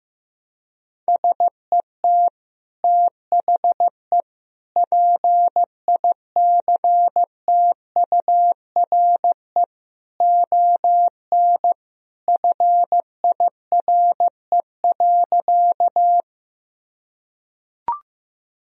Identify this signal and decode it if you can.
Morse code, heard as a keyed tone, about 15 wpm, 700 Hz. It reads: SET THE PICTURE ON FIRE.
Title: Sentences from Top 1000 Words - Review Part 1 - 15wpm